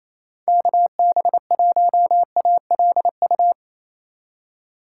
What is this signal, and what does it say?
Morse code, heard as a keyed tone, about 34 wpm, 700 Hz. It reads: KB1ALU
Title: Speed Racing - USA 2x3 Call Signs 28wpm